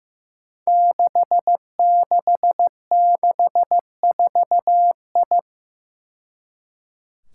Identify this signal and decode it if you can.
Morse code, heard as a keyed tone, about 15 wpm, 700 Hz. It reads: 6664I
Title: Five Challenging Letter-Number Combinations 15wpm